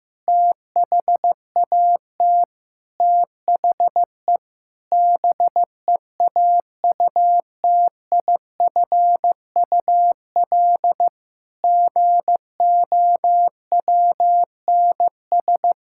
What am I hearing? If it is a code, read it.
Morse code, heard as a keyed tone, about 15 wpm, 700 Hz. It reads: THAT THE BEAUTIFUL GOWNS
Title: Pangram Challenge v2 - 15wpm